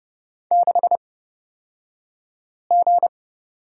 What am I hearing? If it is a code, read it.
Morse code, heard as a keyed tone, about 30 wpm, spaced as for 5 wpm, 700 Hz. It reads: 6G
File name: Lesson 36d - BT - Two Characters 30wpm